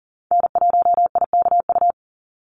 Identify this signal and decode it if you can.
Morse code, heard as a keyed tone, about 40 wpm, 700 Hz. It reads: N1IKU
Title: USA Call Signs 40wpm